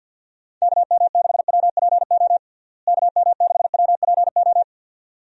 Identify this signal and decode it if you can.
Morse code, heard as a keyed tone, about 50 wpm, 700 Hz. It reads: KM6WPO KM6WPO
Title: USA Call Signs - 6 Characters - Repeated 2x - 50wpm